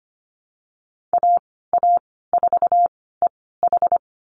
Morse code, heard as a keyed tone, about 25 wpm, spaced as for 16 wpm, 700 Hz. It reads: AA4EH
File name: Lesson 15g - Number 5 - Call Signs 25wpm